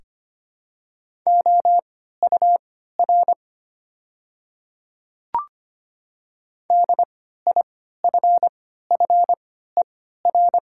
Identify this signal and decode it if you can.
Morse code, heard as a keyed tone, about 25 wpm, spaced as for 14 wpm, 700 Hz. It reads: OUR DIFFER
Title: 3x Character Spacing - Top 200 Words - 25wpm